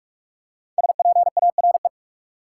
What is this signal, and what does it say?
Morse code, heard as a keyed tone, about 42 wpm, 700 Hz. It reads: SPARE
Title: Five Letter Words - 45wpm